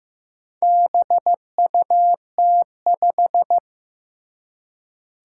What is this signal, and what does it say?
Morse code, heard as a keyed tone, about 15 wpm, 700 Hz. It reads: BUT5
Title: Four Challenging Letter-Number Combinations 15wpm